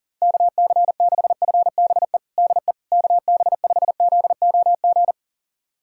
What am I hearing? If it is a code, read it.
Morse code, heard as a keyed tone, about 40 wpm, 700 Hz. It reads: KC6FBE DE KB5ZOG
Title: USA Callsign DE Callsign - 6 Characters - 40wpm